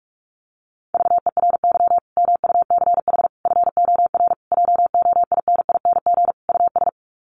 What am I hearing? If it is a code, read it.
Morse code, heard as a keyed tone, about 44 wpm, 700 Hz. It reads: VERY MUCH FOR JOINING US
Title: Sets of 5 Words 45wpm